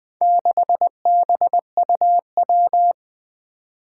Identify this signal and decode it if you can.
Morse code, heard as a keyed tone, about 20 wpm, 700 Hz. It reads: 6BUW
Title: Four Challenging Letter-Number Combinations 20wpm